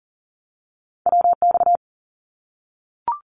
Morse code, heard as a keyed tone, about 40 wpm, 700 Hz. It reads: WX
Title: Abbreviations 40wpm